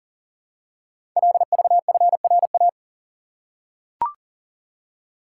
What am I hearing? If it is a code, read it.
Morse code, heard as a keyed tone, about 40 wpm, 690 Hz. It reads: LVFRA
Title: Code Groups - 5 Letters - 40wpm